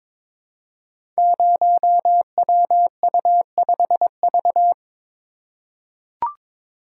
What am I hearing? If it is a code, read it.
Morse code, heard as a keyed tone, about 22 wpm, 690 Hz. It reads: 0WU5V
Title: Code Groups - 5 Letter:Numbers - 22wpm